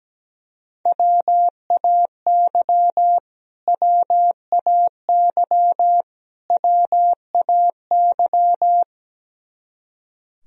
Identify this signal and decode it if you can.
Morse code, heard as a keyed tone, about 17 wpm, 700 Hz. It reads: WAY WAY WAY
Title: Top 100 Words Repeated 17wpm